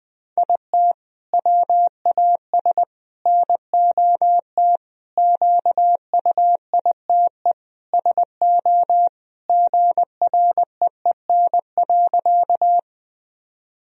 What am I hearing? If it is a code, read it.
Morse code, heard as a keyed tone, about 20 wpm, 700 Hz. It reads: IT WAS NOT QUITE SO GREEN.
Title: Sentences from Top 1000 Words - Review Part 2 - 20wpm